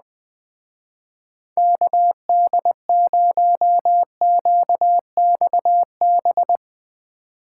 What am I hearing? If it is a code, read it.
Morse code, heard as a keyed tone, about 20 wpm, 700 Hz. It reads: KD0QXB
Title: Speed Racing - USA 2x3 Call Signs 17wpm